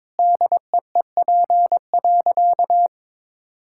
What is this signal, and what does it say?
Morse code, heard as a keyed tone, about 22 wpm, 700 Hz. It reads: DEEP.
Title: Sentences from Top 1600 Words 22wpm